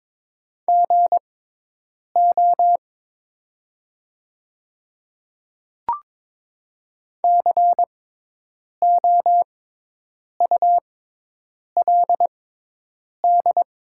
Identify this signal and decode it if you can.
Morse code, heard as a keyed tone, about 22 wpm, spaced as for 8 wpm, 700 Hz. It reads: GO COULD
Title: 6x Character Spacing - Top 100 Words - 22wpm